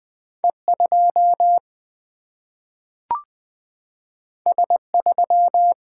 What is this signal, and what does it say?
Morse code, heard as a keyed tone, about 20 wpm, 700 Hz. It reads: E 2 S 3
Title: Single-Letter Single-Number 20wpm